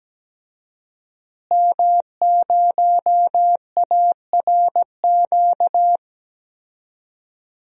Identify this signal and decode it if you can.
Morse code, heard as a keyed tone, about 17 wpm, 700 Hz. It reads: M0ARQ
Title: DXCC-100 Call Signs - 5 Characters - Repeated 2x - 17wpm